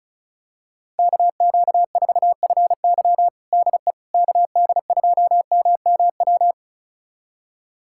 Morse code, heard as a keyed tone, about 36 wpm, 700 Hz. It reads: KQ4FY DE KD2MMW
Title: USA Callsign DE Callsign 35wpm